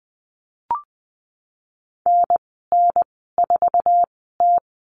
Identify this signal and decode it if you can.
Morse code, heard as a keyed tone, about 20 wpm, spaced as for 14 wpm, 700 Hz. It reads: NN4T
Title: 048 Lesson 9g - Number 4 - Call Signs 20wpm